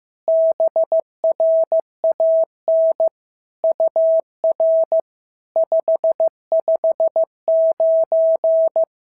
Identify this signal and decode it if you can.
Morse code, heard as a keyed tone, about 15 wpm, 650 Hz. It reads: BRAN UR 559
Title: SOTA QSO Practice 15wpm